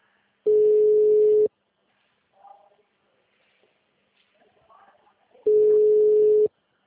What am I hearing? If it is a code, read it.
Ringback tone (Europe, 425 Hz).